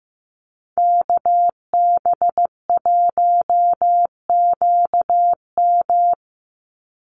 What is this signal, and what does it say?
Morse code, heard as a keyed tone, about 15 wpm, 700 Hz. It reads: KB1QM
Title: USA Call Signs 15wpm